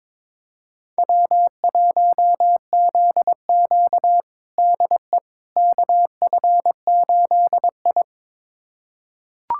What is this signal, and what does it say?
Morse code, heard as a keyed tone, about 22 wpm, 700 Hz. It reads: W1ZQ DE KF8I
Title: USA Callsign DE Callsign - 4 Characters - 22wpm